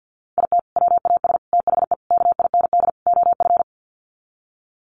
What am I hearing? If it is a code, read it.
Morse code, heard as a keyed tone, about 50 wpm, 700 Hz. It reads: IT WAS THE KIND OF
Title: Sets of 5 Words Encore2 50wpm